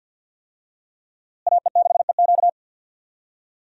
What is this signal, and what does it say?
Morse code, heard as a keyed tone, about 52 wpm, 700 Hz. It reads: AE6EQ
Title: USA Call Signs - 5 Characters - Repeated 3x - 50wpm